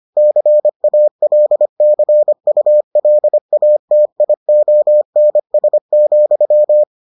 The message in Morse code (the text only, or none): CALCULATIONS,